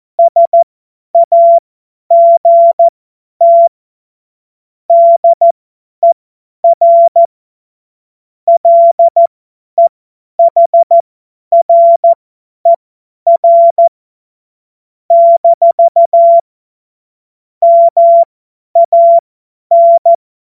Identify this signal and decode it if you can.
Morse code, heard as a keyed tone, about 14 wpm, spaced as for 10 wpm, 680 Hz. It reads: SAGT DER LEHRER - MAN
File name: Morsekurs / Witze / Witz 05. Oktober 2021, 50 F70